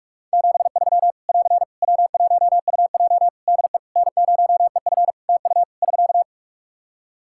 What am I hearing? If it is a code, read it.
Morse code, heard as a keyed tone, about 41 wpm, 700 Hz. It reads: 73 <AR> W1UJ DE N0EF TU <SK>
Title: Practice QSO - Exchange 4 - Ending - 45wpm